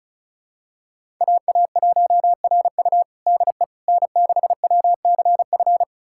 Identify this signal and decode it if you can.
Morse code, heard as a keyed tone, about 34 wpm, 700 Hz. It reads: AA1RU DE N6WCF
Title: USA Callsign DE Callsign - 5 Characters - 35wpm